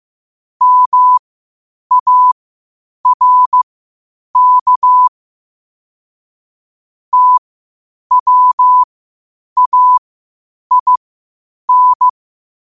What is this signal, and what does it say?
Morse code, heard as a keyed tone, about 15 wpm, spaced as for 8 wpm, 1000 Hz. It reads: MARK TWAIN
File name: qotdfiles / 8wpm